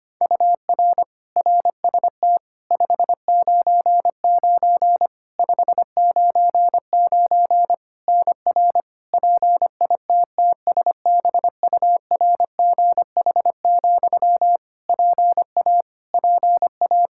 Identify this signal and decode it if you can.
Morse code, heard as a keyed tone, about 25 wpm, 700 Hz. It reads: UR RST 599 599 NR PITTSBURGH, PA PA
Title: Practice QSO - Exchange 1 - Beginning - 25wpm